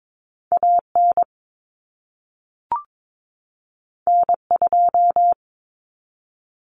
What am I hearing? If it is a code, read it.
Morse code, heard as a keyed tone, about 22 wpm, 700 Hz. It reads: AN N2
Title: Common Cut Numbers - 2-Digit 22wpm